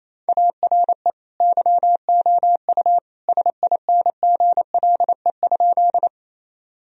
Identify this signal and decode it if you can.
Morse code, heard as a keyed tone, about 27 wpm, 700 Hz. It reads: ARE YOU SINGLE?
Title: Sentences from Top 1000 Words - Review Part 4 - 28wpm